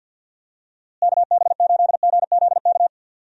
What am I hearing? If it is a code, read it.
Morse code, heard as a keyed tone, about 50 wpm, 700 Hz. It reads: KB8GZK
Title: USA Call Signs - 6 Characters - Repeated 2x - 50wpm